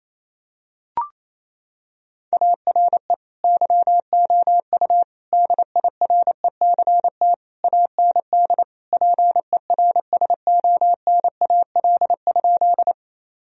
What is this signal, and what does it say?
Morse code, heard as a keyed tone, about 27 wpm, 700 Hz. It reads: ARE YOU DIRECT AND PERSONAL?
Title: Sentences from Top 1000 Words - Review Part 4 - 28wpm